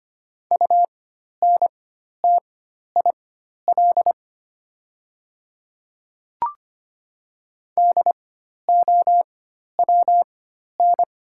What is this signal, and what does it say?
Morse code, heard as a keyed tone, about 25 wpm, spaced as for 12 wpm, 700 Hz. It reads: UNTIL DOWN
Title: 4x Character Spacing - Top 500 Words - 25wpm